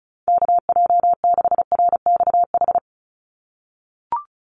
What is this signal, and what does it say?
Morse code, heard as a keyed tone, about 34 wpm, 700 Hz. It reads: KJ6RXH
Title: USA Call Signs 35wpm